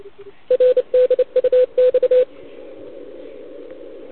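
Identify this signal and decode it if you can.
Morse code, heard as a keyed tone, about 29 wpm, 500 Hz, with radio noise. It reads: RDUX
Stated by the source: CW from the OKOM DX contest on 80 m, received with a TS480.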